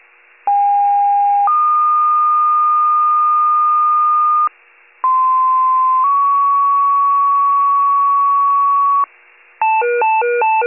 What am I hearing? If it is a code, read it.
Two-tone page, heard as a one-second tone, then a three-second tone, about 810 then 1220 Hz, about 1040 then 1120 Hz. It is heard from 0.5 to 9.1 s.